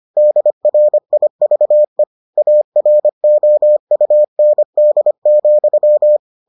Morse code, heard as a keyed tone, about 25 wpm, 600 Hz. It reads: DRIVE AROUND,